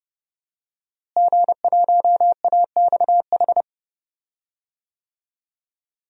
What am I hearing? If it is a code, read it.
Morse code, heard as a keyed tone, about 30 wpm, 700 Hz. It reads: G1AXH
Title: DXCC-100 Call Signs - 5 Characters - Repeated 2x - 30wpm